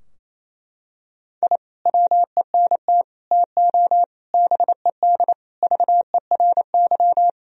Morse code, heard as a keyed tone, about 28 wpm, 700 Hz. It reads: I WENT TO BED VERY
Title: Sentences from Top 1000 Words - Review Part 2 - 28wpm